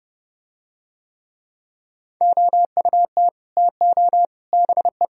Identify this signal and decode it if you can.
Morse code, heard as a keyed tone, about 30 wpm, 700 Hz. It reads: OUT TO BE